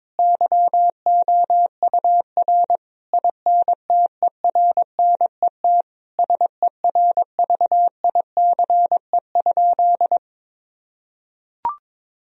Morse code, heard as a keyed tone, about 22 wpm, 700 Hz. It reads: YOUR INTERNET SERVICE?
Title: Sentences from Top 1000 Words - Review Part 4 - 22wpm